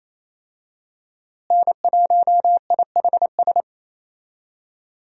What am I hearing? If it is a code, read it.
Morse code, heard as a keyed tone, about 27 wpm, 700 Hz. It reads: N1IHS
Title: USA Call Signs - 5 Characters - Repeated 3x - 28wpm